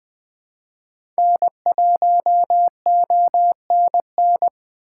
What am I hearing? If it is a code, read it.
Morse code, heard as a keyed tone, about 20 wpm, 700 Hz. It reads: N1ONN
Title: USA Call Signs - 5 Characters - 3x Character Spacing - 20wpm